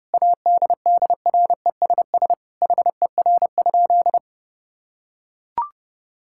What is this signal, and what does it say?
Morse code, heard as a keyed tone, about 30 wpm, 700 Hz. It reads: ADDRESS HER?
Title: Sentences from Top 1000 Words - Review Part 6 - 30wpm